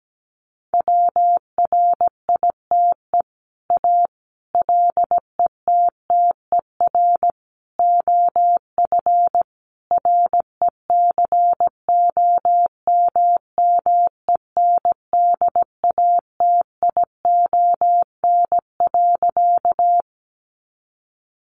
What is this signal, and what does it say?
Morse code, heard as a keyed tone, about 17 wpm, 700 Hz. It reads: WRITE A LETTER OF RECOMMENDATION.
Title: Sentences from Top 1400 Words 17wpm